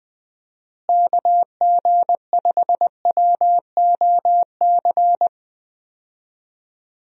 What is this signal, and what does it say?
Morse code, heard as a keyed tone, about 19 wpm, 700 Hz. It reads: KG5WOC
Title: USA Call Signs - 6 Characters - Repeated 3x - 20wpm